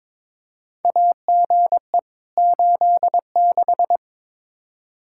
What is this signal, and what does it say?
Morse code, heard as a keyed tone, about 22 wpm, 700 Hz. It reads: AGE 86
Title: K1USN SST QSO Practice 22wpm